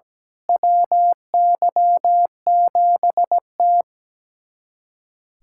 Morse code, heard as a keyed tone, about 17 wpm, 700 Hz. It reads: WY7T
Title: USA Callsign DE Callsign - 4 Characters - 17wpm